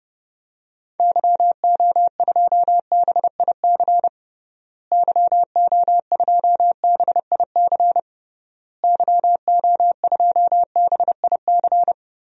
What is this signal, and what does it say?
Morse code, heard as a keyed tone, about 30 wpm, 700 Hz. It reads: YO2BIC YO2BIC YO2BIC
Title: DXCC-100 Call Signs - 6 Characters - Repeated 3x - 30wpm